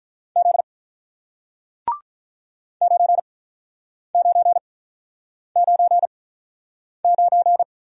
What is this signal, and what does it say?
Morse code, heard as a keyed tone, about 48 wpm, spaced as for 9 wpm, 700 Hz. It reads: Z 9999